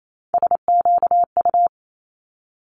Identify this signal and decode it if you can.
Morse code, heard as a keyed tone, about 28 wpm, 700 Hz. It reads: SQU
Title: Multi-letter Phonemes Quick Fire 28wpm